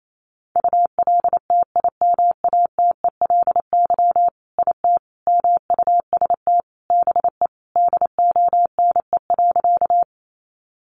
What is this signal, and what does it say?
Morse code, heard as a keyed tone, about 28 wpm, 700 Hz. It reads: ULTIMATELY IT MUST BE DONE.
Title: Sentences from Top 1200 Words 28wpm